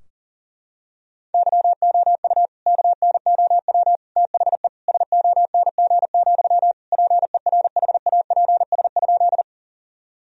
Morse code, heard as a keyed tone, about 40 wpm, 700 Hz. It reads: YOU KNOW THE SONG, PERHAPS?
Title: Sentences from Top 1000 Words - Review Part 2 - 40wpm